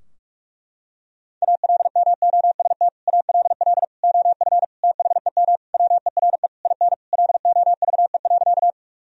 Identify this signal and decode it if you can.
Morse code, heard as a keyed tone, about 41 wpm, 700 Hz. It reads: ALMOST ALL OF THEM WERE IN LOVE.
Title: Sentences from Top 1000 Words - Review Part 4 - 45wpm